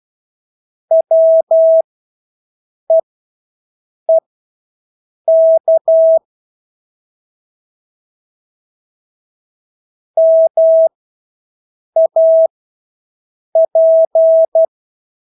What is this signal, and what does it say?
Morse code, heard as a keyed tone, about 12 wpm, spaced as for 6 wpm, 650 Hz. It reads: WEEK MAP
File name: Engelske ord 12-6 wpm